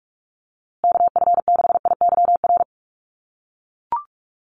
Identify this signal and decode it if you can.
Morse code, heard as a keyed tone, about 44 wpm, 700 Hz. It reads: KF6IYR
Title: USA Call Signs Repeated 3x 45wpm